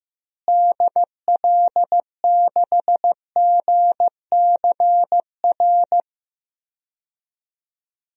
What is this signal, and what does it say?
Morse code, heard as a keyed tone, about 16 wpm, 700 Hz. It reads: DL6GCR
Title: DXCC-100 Call Signs - 6 Characters - Repeated 3x - 15wpm